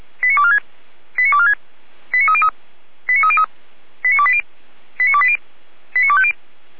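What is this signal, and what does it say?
Five-tone selective call (ZVEI), heard as quick runs of five tones, tones of about 70 ms. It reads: 89246 89246 89303 89303 89280 89280 89250
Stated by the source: ZVEI five-tone call received by SDR.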